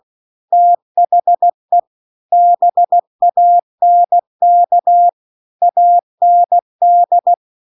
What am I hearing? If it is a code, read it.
Morse code, heard as a keyed tone, about 16 wpm, 700 Hz. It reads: THE BANK AND
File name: CW-The-Big-Twoheaded-River-2-16WPM